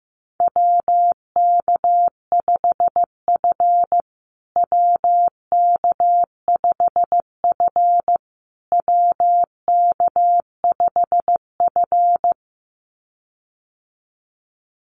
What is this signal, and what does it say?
Morse code, heard as a keyed tone, about 15 wpm, 700 Hz. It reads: WK5F WK5F WK5F
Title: USA Call Signs Repeated 3x 15wpm